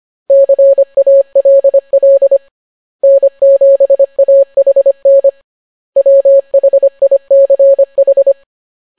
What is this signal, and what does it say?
Morse code, heard as a keyed tone, about 25 wpm, 550 Hz. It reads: CALL N7AHN WHICH